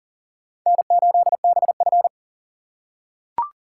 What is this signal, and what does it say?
Morse code, heard as a keyed tone, about 40 wpm, 700 Hz. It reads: N8BF
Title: USA Call Signs - 4 Characters - 6x Character Spacing - 40wpm